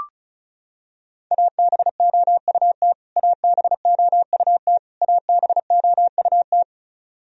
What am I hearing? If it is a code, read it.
Morse code, heard as a keyed tone, about 34 wpm, 700 Hz. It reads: ABOUT ABOUT ABOUT
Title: Top 100 Words Repeated 35wpm